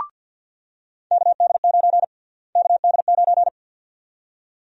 Morse code, heard as a keyed tone, about 50 wpm, 700 Hz. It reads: KD9 KD9
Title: USA Call Sign Prefixes - Repeated 2x - 50wpm